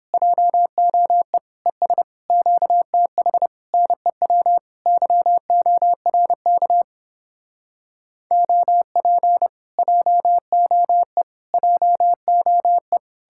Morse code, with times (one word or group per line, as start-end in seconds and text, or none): 0.1-1.4 JOE
1.7-2.0 ES
2.3-3.5 QTH
3.7-4.6 NEW
4.9-6.8 YORK
8.3-9.5 OP
9.8-11.2 JOE
11.5-13.0 JOE